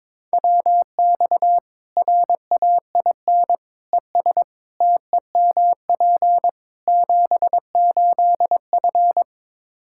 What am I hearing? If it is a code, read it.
Morse code, heard as a keyed tone, about 22 wpm, 700 Hz. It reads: WX RAIN ES TEMP 78F
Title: K1USN SST QSO Practice 22wpm